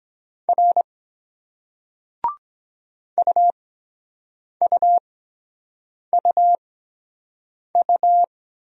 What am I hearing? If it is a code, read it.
Morse code, heard as a keyed tone, about 22 wpm, spaced as for 7 wpm, 700 Hz. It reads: R UUUU